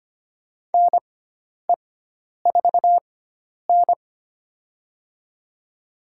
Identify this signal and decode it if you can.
Morse code, heard as a keyed tone, about 25 wpm, spaced as for 10 wpm, 700 Hz. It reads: NE4N
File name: Lesson 16g - Letter c - Call Signs 25wpm